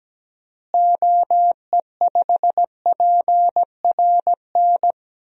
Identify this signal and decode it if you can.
Morse code, heard as a keyed tone, about 17 wpm, 700 Hz. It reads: OE5PRN
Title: DXCC-100 Call Signs - 6 Characters - Repeated 3x - 17wpm